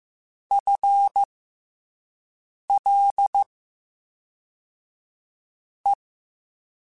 Morse code, heard as a keyed tone, about 15 wpm, spaced as for 4 wpm, 800 Hz. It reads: FLE
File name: FL / FL mp3s / FL Prefixes 3x2 sp @5